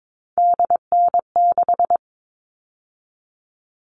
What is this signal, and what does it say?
Morse code, heard as a keyed tone, about 22 wpm, 700 Hz. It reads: DN6
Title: All Cut Numbers - 3-Digit 22wpm